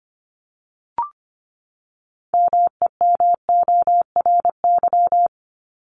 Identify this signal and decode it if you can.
Morse code, heard as a keyed tone, about 25 wpm, 700 Hz. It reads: MEMORY
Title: Top 1251-1500 Words 25wpm